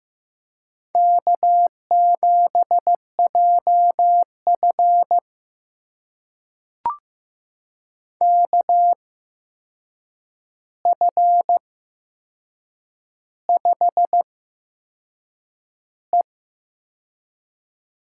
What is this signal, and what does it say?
Morse code, heard as a keyed tone, about 15 wpm, spaced as for 6 wpm, 700 Hz. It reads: K7JF KF5E
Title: USA Call Signs - 4 Characters - 8x Character Spacing - 15wpm